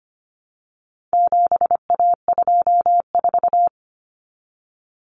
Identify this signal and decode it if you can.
Morse code, heard as a keyed tone, about 25 wpm, 700 Hz. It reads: 7A24
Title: Common Cut Numbers - 4-Digit 25wpm